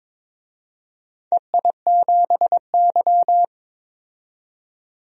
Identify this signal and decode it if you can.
Morse code, heard as a keyed tone, about 22 wpm, 700 Hz. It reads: EI7Y
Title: DXCC-100 Call Signs - 4 Characters - 22wpm